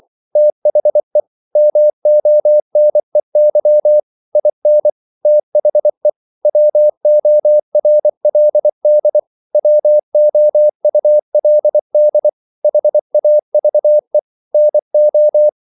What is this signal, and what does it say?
Morse code, heard as a keyed tone, about 24 wpm, 600 Hz. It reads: THE MONEY IN THE WORLD WOULD HAVE NO
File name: CW / Practice / EnglishQuotations0324